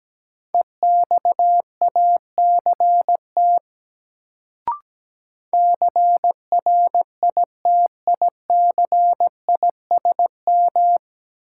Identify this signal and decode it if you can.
Morse code, heard as a keyed tone, about 17 wpm, 700 Hz. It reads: EXACT CRITICISM
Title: Sentences from Top 1300 Words - New Word Drill - 17wpm